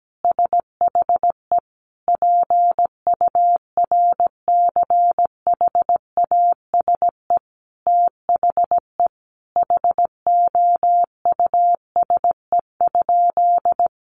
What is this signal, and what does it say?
Morse code, heard as a keyed tone, about 17 wpm, 700 Hz. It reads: SHE PURCHASE THE HOUSE?
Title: Sentences from Top 800 Words 17wpm